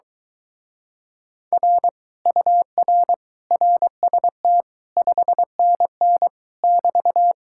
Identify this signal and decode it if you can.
Morse code, heard as a keyed tone, about 23 wpm, 700 Hz. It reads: R UR RST 5NN <BT>